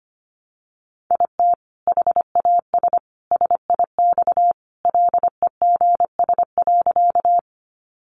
Morse code, heard as a keyed tone, about 25 wpm, 700 Hz. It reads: IT HAS SIX LEGS.
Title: Additional Easy Phrases 25wpm